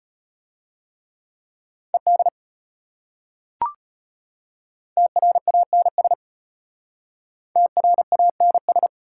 Morse code, heard as a keyed tone, about 33 wpm, 700 Hz. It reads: ED TRANS TRANS